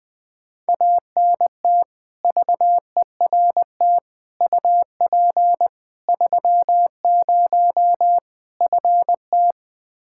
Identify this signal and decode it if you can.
Morse code, heard as a keyed tone, about 20 wpm, 700 Hz. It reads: ANT VERT UP 30 FT
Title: K1USN SST QSO Practice 20wpm